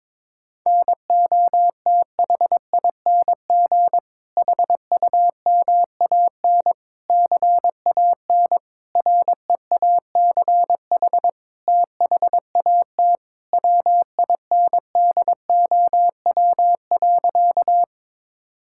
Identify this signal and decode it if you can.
Morse code, heard as a keyed tone, about 22 wpm, 700 Hz. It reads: NOTHING HUMAN CAN REACH THAT WINDOW.
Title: Sentences from Top 1600 Words 22wpm